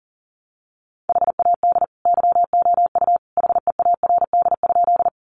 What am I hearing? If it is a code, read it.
Morse code, heard as a keyed tone, about 40 wpm, 700 Hz. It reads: HAD YOU HEARD?
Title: Sentences from Top 400 Words 40wpm